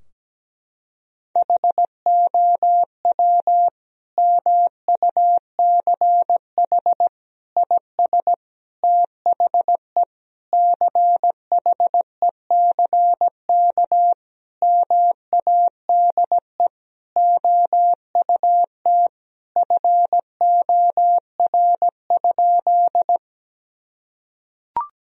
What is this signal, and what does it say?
Morse code, heard as a keyed tone, about 17 wpm, 700 Hz. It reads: HOW MUCH IS THE CHECK MADE OUT FOR?
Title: Sentences from Top 1000 Words - Review Part 3 - 17wpm